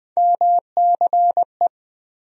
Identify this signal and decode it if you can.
Morse code, heard as a keyed tone, about 21 wpm, 700 Hz. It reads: MCE
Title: USA Call Sign Postfixes - 3 Letters 20wpm